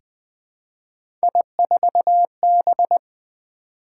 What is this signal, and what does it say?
Morse code, heard as a keyed tone, about 20 wpm, 700 Hz. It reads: I4B
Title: Three Letter-Number Combinations - 20wpm